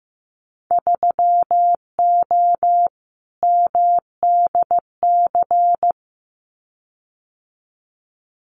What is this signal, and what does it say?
Morse code, heard as a keyed tone, about 15 wpm, 700 Hz. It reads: 3O MDC
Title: Winter Field Day Exchanges Encore 15wpm